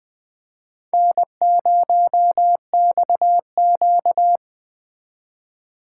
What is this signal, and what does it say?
Morse code, heard as a keyed tone, about 24 wpm, 700 Hz. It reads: N0XQ
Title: Speed Racing - USA Call Signs 20wpm